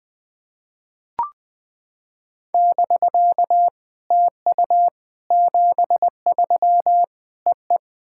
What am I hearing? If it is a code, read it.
Morse code, heard as a keyed tone, about 19 wpm, 700 Hz. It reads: <BK> TU 73 EE
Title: SOTA QSO Practice 20wpm